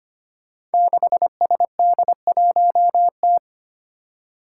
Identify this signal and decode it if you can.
Morse code, heard as a keyed tone, about 25 wpm, 700 Hz. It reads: 6SD1T
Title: Five Challenging Letter-Number Combinations 25wpm